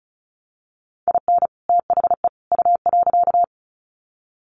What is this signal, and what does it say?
Morse code, heard as a keyed tone, about 34 wpm, 700 Hz. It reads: IN THE U.
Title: Sets of 3 Words 35wpm